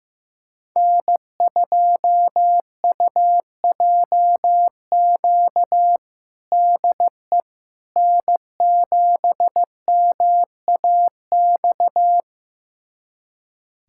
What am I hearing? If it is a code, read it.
Morse code, heard as a keyed tone, about 15 wpm, 700 Hz. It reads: N2UJQ DE N7MAX
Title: USA Callsign DE Callsign - 5 Characters - 15wpm